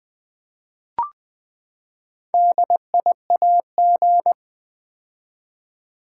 Morse code, heard as a keyed tone, about 20 wpm, 700 Hz. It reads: DIAG
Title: Build A Word - 4 Letters At A Time - 8-Letter Words - 20wpm